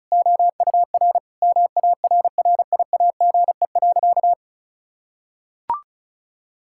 Morse code, heard as a keyed tone, about 36 wpm, 700 Hz. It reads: OUR MARRIAGE.
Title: Sentences from Top 1000 Words - Review Part 5 - 35wpm